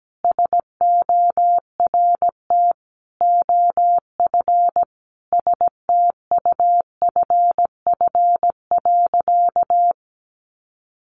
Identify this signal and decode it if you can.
Morse code, heard as a keyed tone, about 17 wpm, 700 Hz. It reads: SORT OF STUFF.
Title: Sentences from Top 1400 Words 17wpm